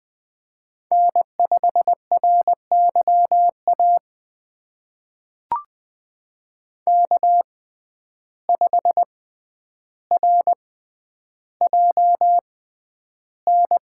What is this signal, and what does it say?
Morse code, heard as a keyed tone, about 20 wpm, spaced as for 10 wpm, 700 Hz. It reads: N5RYA K5RJN
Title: USA Call Signs - 5 Characters - 6x Character Spacing - 20wpm